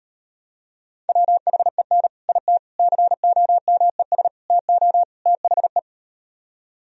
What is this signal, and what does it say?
Morse code, heard as a keyed tone, about 38 wpm, 700 Hz. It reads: WHEN IT COMES TO THE